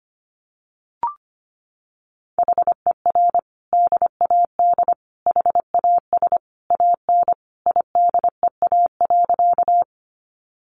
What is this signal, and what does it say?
Morse code, heard as a keyed tone, about 25 wpm, 700 Hz. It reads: HER DAD HAS AN IDEA.
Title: Additional Easy Phrases 25wpm